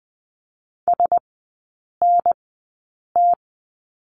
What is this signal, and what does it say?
Morse code, heard as a keyed tone, about 20 wpm, spaced as for 8 wpm, 700 Hz. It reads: SNT
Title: 041 Lesson 8e - Number 1 - Three Characters 20wpm